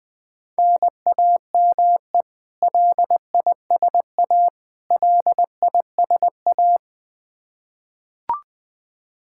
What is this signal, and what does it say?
Morse code, heard as a keyed tone, about 21 wpm, 700 Hz. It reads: NAME LISA LISA
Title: Practice QSO - Exchange 1 - Beginning - 20wpm